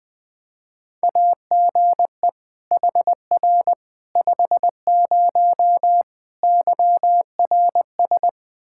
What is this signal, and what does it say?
Morse code, heard as a keyed tone, about 20 wpm, 700 Hz. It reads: AGE HR 50 YRS
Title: K1USN SST QSO Practice 20wpm